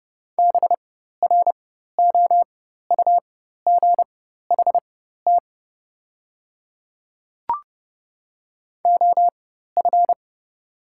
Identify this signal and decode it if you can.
Morse code, heard as a keyed tone, about 30 wpm, spaced as for 14 wpm, 700 Hz. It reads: BROUGHT OF
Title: 4x Character Spacing - Top 500 Words - 30wpm